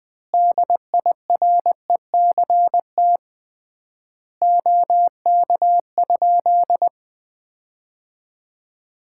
Morse code, heard as a keyed tone, about 20 wpm, 700 Hz. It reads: DIRECT OK?
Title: Common QSO Phrases - 3x Word-Spacing - 20wpm